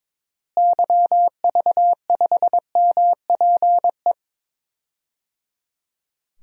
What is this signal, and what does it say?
Morse code, heard as a keyed tone, about 22 wpm, 700 Hz. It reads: YV5MPE
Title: DXCC-100 Call Signs - 6 Characters - Repeated 3x - 22wpm